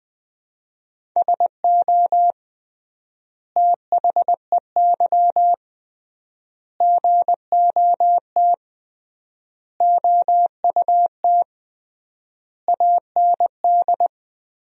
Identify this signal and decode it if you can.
Morse code, heard as a keyed tone, about 21 wpm, 700 Hz. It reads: SO THEY GOT OUT AND
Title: 3x Word-Spacing - Sentences from Top 300 Words - 20wpm